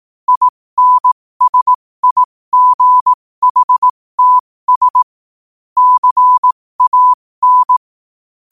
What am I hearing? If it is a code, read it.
Morse code, heard as a keyed tone, about 18 wpm, spaced as for 15 wpm, 1000 Hz. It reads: INSIGHTS CAN
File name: qotdfiles / 15wpm